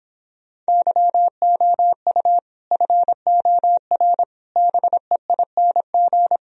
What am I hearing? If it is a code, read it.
Morse code, heard as a keyed tone, about 26 wpm, 700 Hz. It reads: YOU FOR BEING